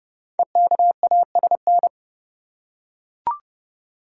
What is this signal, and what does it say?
Morse code, heard as a keyed tone, about 30 wpm, 700 Hz. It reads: EKASN
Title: Code Groups - 5 Letters - 30wpm